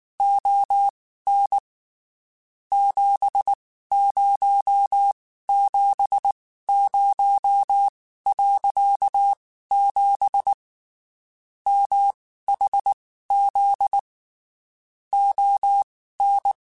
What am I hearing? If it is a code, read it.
Morse code, heard as a keyed tone, about 19 wpm, spaced as for 14 wpm, 800 Hz. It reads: ON 7070.7 MHZ ON
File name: (spaced) / QSOs Spaced-mp3s / kc7haa @18 sp